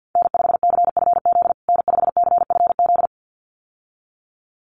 Morse code, heard as a keyed tone, about 50 wpm, 700 Hz. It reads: N5CFZ N5CFZ
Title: USA Call Signs Repeated 2x 50wpm